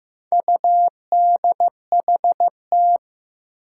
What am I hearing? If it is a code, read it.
Morse code, heard as a keyed tone, about 15 wpm, 700 Hz. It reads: UDHT
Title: Four Challenging Letter-Number Combinations 15wpm